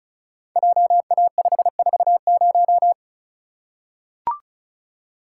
Morse code, heard as a keyed tone, about 36 wpm, 700 Hz. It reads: JA540
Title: Code Groups - 5 Letter:Numbers - 35wpm